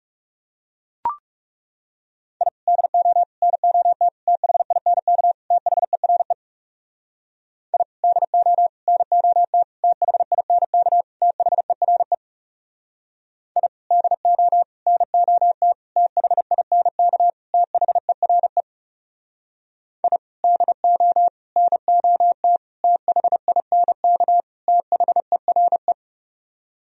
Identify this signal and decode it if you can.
Morse code, heard as a keyed tone, about 36 wpm, 700 Hz. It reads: I DO NOT THINK THERE I DO NOT THINK THERE I DO NOT THINK THERE I DO NOT THINK THERE
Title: Speed-Racing - Sets of 5 Words 30wpm